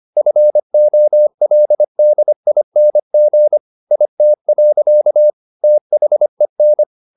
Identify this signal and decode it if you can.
Morse code, heard as a keyed tone, about 25 wpm, 600 Hz. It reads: FOLDING IT. THEN
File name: the old man and the sea part 1 25wpm